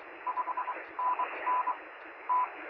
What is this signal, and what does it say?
Morse code, heard as a keyed tone, about 23 wpm, spaced as for 19 wpm, 1000 Hz, with radio noise. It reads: 5NN T
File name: sm7lcb / 0630z yl2aj 23cm